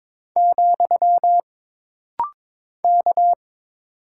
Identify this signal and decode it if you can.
Morse code, heard as a keyed tone, about 22 wpm, 700 Hz. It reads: , K
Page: This rapid-fire timing is geared to push your ICR (Instant Character Recognition) skills.